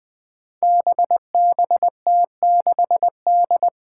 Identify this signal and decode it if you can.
Morse code, heard as a keyed tone, about 20 wpm, 700 Hz. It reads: BBT6D
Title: Five Challenging Letter-Number Combinations 20wpm